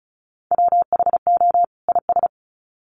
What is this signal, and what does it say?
Morse code, heard as a keyed tone, about 34 wpm, 700 Hz. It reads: WHO IS
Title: Sets of 2 Words 35wpm